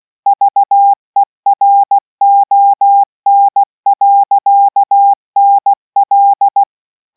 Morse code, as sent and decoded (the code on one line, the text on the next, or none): ...- . .-. --- -. .-.-.- -. .-..
VERON.NL